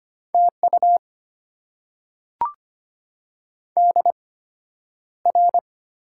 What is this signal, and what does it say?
Morse code, heard as a keyed tone, about 25 wpm, spaced as for 11 wpm, 690 Hz. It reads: TU DR
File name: Top 90 QSO Elements - 8x Character Spacing - 25wpm